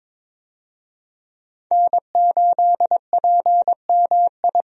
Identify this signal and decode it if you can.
Morse code, heard as a keyed tone, about 22 wpm, 700 Hz. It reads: N8PMI
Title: USA Call Signs - 5 Characters - Repeated 3x - 22wpm